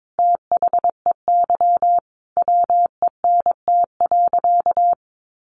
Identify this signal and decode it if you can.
Morse code, heard as a keyed tone, about 22 wpm, 700 Hz. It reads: THEY WENT.
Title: Sentences from Top 1300 Words 22wpm